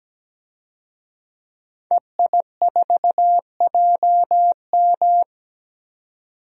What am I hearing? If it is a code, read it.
Morse code, heard as a keyed tone, about 17 wpm, 700 Hz. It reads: EI4JM
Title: DXCC-100 Call Signs - 5 Characters - 17wpm